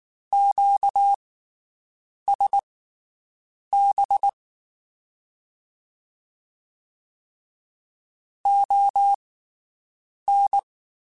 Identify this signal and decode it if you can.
Morse code, heard as a keyed tone, about 19 wpm, spaced as for 7 wpm, 800 Hz. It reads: QSB ON
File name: (spaced) / QSOs Spaced-mp3s / WD0BGP 1 @18 sp